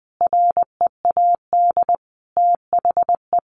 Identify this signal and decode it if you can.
Morse code, heard as a keyed tone, about 20 wpm, 700 Hz. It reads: READ THE
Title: Sentences from Top 300 Words 20wpm